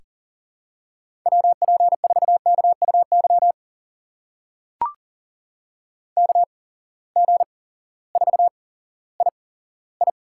Morse code, heard as a keyed tone, about 40 wpm, spaced as for 18 wpm, 700 Hz. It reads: WP4KUY KG4II
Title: USA Call Signs - 6 Characters - 8x Character Spacing - 40wpm